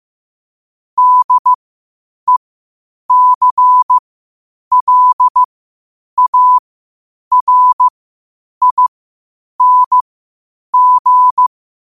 Morse code, heard as a keyed tone, about 15 wpm, spaced as for 8 wpm, 1000 Hz. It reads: DECLARING